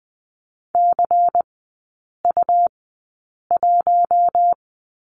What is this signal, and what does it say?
Morse code, heard as a keyed tone, about 20 wpm, spaced as for 8 wpm, 700 Hz. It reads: CU1
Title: 159 Lesson 26e - Letter g - Three Characters 20wpm